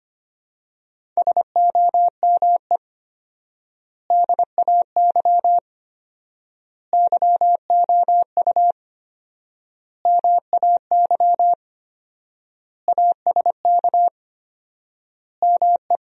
Morse code, heard as a keyed tone, about 25 wpm, 700 Hz. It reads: SOME DAY YOU MAY ASK ME
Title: 4x Word-Spacing - Sentences from Top 200 Words - 25wpm